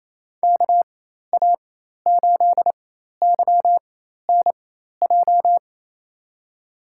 Morse code, heard as a keyed tone, about 28 wpm, spaced as for 13 wpm, 700 Hz. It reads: KA8YNJ
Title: USA Call Signs - 4x Character Spacing - 28wpm